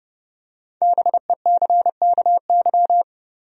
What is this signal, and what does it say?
Morse code, heard as a keyed tone, about 30 wpm, 700 Hz. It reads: BECKY
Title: USA Common Names - Max 5 Letters - 30wpm